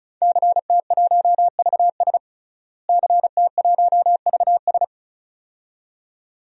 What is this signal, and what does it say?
Morse code, heard as a keyed tone, about 36 wpm, 700 Hz. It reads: CT1VS CT1VS
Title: DXCC-100 Call Signs - 5 Characters - Repeated 2x - 35wpm